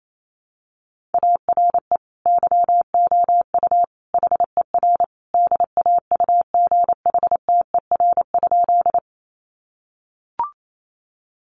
Morse code, heard as a keyed tone, about 28 wpm, 700 Hz. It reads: ARE YOU HER DAUGHTER?
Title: Sentences from Top 1500 Words 28wpm